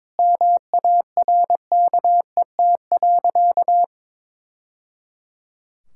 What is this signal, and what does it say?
Morse code, heard as a keyed tone, about 22 wpm, 700 Hz. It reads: MARKET.
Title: Sentences from Top 1000 Words - Review Part 6 - 22wpm